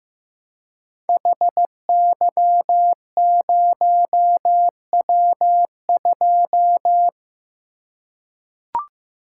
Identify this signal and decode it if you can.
Morse code, heard as a keyed tone, about 15 wpm, 700 Hz. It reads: HY0W2
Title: Code Groups - 5 Letter:Numbers - 15wpm